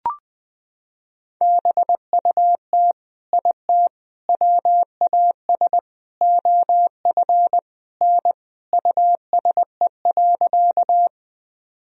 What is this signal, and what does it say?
Morse code, heard as a keyed tone, about 21 wpm, 700 Hz. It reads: BUT IT WAS OF N USE.
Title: Sentences from Top 100 Words and Abbreviations 20wpm